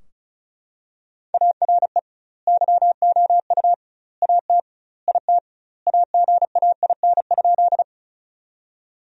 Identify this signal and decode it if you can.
Morse code, heard as a keyed tone, about 36 wpm, 700 Hz. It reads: ARE YOU AT IT AGAIN?
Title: 2x Word-Spacing - Sentences from Top 200 Words - 35wpm